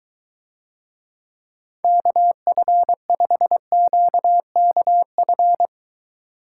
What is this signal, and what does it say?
Morse code, heard as a keyed tone, about 23 wpm, 700 Hz. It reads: KF5QKF